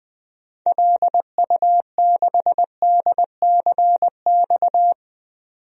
Morse code, heard as a keyed tone, about 20 wpm, 700 Hz. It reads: LU6DCX
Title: DXCC-100 Call Signs - 6 Characters - 20wpm